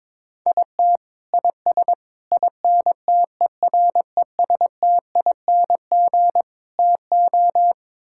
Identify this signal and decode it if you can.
Morse code, heard as a keyed tone, about 22 wpm, 700 Hz. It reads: IT IS INTERESTING TO
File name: practice / 22wpm / Sets of 6 Words - 22wpm